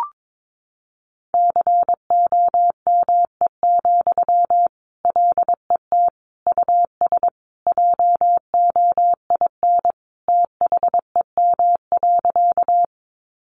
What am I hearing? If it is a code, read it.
Morse code, heard as a keyed tone, about 22 wpm, 700 Hz. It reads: COME, LET US JOIN THEM.
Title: Sentences from Top 1400 Words 22wpm